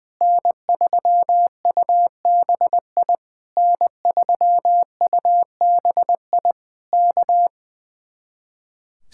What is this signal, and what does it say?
Morse code, heard as a keyed tone, about 20 wpm, 700 Hz. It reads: N3UBI N3UBI K
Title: Speed-Racing Common QSO Phrases Encore - 20wpm